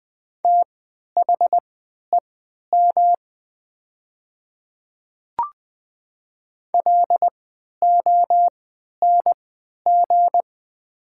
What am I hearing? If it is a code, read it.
Morse code, heard as a keyed tone, about 20 wpm, spaced as for 11 wpm, 700 Hz. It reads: THEM LONG
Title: 3x Character Spacing - Top 100 Words - 20wpm